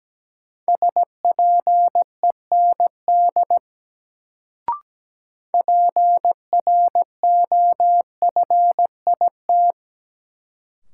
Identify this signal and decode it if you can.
Morse code, heard as a keyed tone, about 17 wpm, 700 Hz. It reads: SPEND PROFIT
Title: Sentences from Top 800 Words - New Word Drill - 17wpm